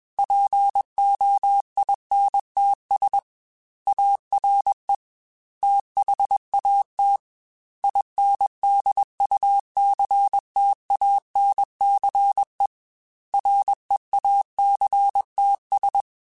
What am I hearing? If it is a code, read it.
Morse code, heard as a keyed tone, about 21 wpm, 800 Hz. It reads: POINTS ARE THAT INDUCTANCE REACTS
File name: Technical Subjects / Technical Subjects-mp3s / Reactance @20